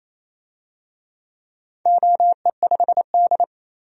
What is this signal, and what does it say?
Morse code, heard as a keyed tone, about 27 wpm, 700 Hz. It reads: OE5D
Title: DXCC-100 Call Signs - 4 Characters - 28wpm